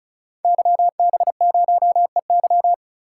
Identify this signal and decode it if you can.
Morse code, heard as a keyed tone, about 36 wpm, 700 Hz. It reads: YB0EY
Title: Code Groups - 5 Letter:Numbers - 35wpm